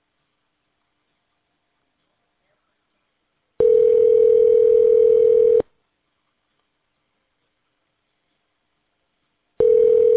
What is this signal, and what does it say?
Ringback tone (North America), 440 + 480 Hz.